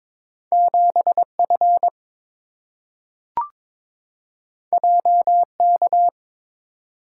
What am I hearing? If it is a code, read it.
Morse code, heard as a keyed tone, about 22 wpm, 700 Hz. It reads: 7F JK
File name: Two Letter-Number Combinations - 22wpm